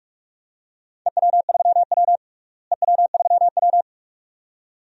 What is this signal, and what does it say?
Morse code, heard as a keyed tone, about 41 wpm, 700 Hz. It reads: EW3W EW3W
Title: DXCC-100 Call Signs - 4 Characters - Repeated 2x - 45wpm